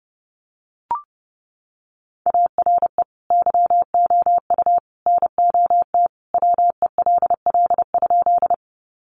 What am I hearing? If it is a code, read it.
Morse code, heard as a keyed tone, about 30 wpm, 700 Hz. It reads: ARE YOU NOT WELL?
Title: Sentences from Top 600 Words 30wpm